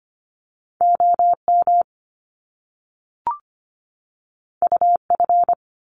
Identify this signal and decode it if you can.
Morse code, heard as a keyed tone, about 25 wpm, 700 Hz. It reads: OM UF
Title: Top Two Letter Combinations in Words 25wpm